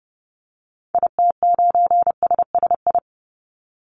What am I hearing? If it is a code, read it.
Morse code, heard as a keyed tone, about 30 wpm, 700 Hz. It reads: IT9SSI
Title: Call Signs Repeated 2x - 30wpm